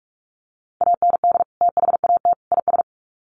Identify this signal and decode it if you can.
Morse code, heard as a keyed tone, about 44 wpm, 700 Hz. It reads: AND THAT IS
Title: Sets of 3 Words 45wpm